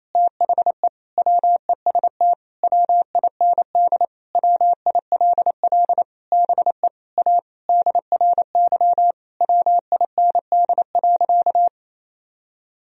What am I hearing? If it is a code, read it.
Morse code, heard as a keyed tone, about 28 wpm, 700 Hz. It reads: THE WEST WIND WILL BE A DRY WIND.
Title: Sentences from Top 1000 Words - Review Part 3 - 28wpm